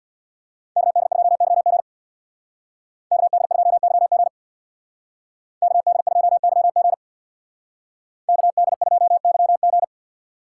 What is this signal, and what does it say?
Morse code, heard as a keyed tone, about 64 wpm, 700 Hz. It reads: KD2YG KD2YG KD2YG KD2YG